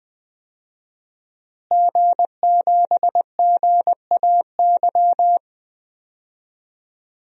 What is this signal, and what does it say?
Morse code, heard as a keyed tone, about 21 wpm, 700 Hz. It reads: G7GAY
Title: DXCC-100 Call Signs - 5 Characters - Repeated 3x - 20wpm